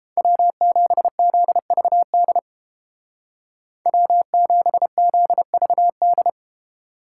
Morse code, heard as a keyed tone, about 30 wpm, 700 Hz. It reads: W7ZVD W7ZVD